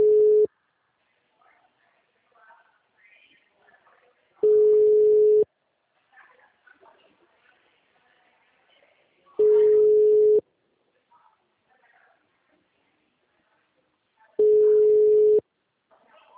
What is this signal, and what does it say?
Ringback tone (Europe, 425 Hz).
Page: Dracena (Call-Center)